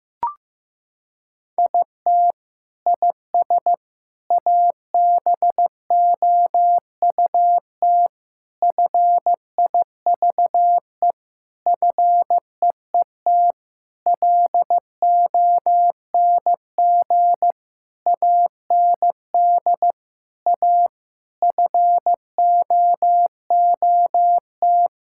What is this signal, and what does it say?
Morse code, heard as a keyed tone, about 15 wpm, 700 Hz. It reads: IT IS ABOUT FIVE FEET LONG AND A FOOT
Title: Sentences from Top 1000 Words - Review Part 6 - 15wpm